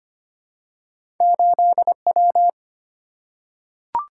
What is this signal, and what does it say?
Morse code, heard as a keyed tone, about 25 wpm, 700 Hz. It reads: 8W
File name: Two Letter-Number Combinations - 25wpm